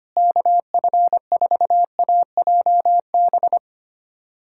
Morse code, heard as a keyed tone, about 25 wpm, 700 Hz. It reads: KF4AJB
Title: USA Callsign DE Callsign 25wpm